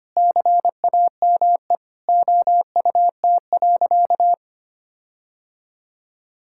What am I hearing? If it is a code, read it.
Morse code, heard as a keyed tone, about 25 wpm, 700 Hz. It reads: CAME OUT.
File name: practice / 25wpm / Sentences with 9-Word Phrases - 25wpm